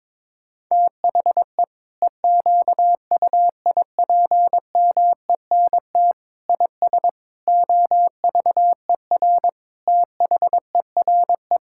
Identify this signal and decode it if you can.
Morse code, heard as a keyed tone, about 22 wpm, 700 Hz. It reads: THE EQUIPMENT IS OVER THERE
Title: Sentences from Top 1000 Words - Review Part 5 - 22wpm